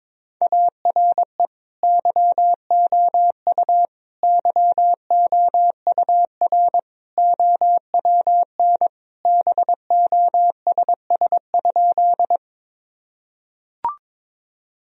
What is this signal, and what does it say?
Morse code, heard as a keyed tone, about 22 wpm, 700 Hz. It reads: ARE YOU YOUR OWN BOSS?
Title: Sentences from Top 1000 Words - Review Part 4 - 22wpm